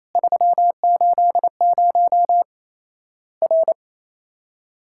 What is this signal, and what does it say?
Morse code, heard as a keyed tone, about 28 wpm, 700 Hz. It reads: 380 R
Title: Worked All Europe DX Contest - QTC Exchange - 28wpm